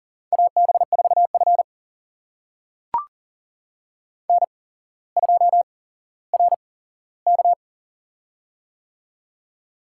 Morse code, heard as a keyed tone, about 38 wpm, spaced as for 17 wpm, 700 Hz. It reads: AB4F N2RK
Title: USA Call Signs - 4 Characters - 8x Character Spacing - 40wpm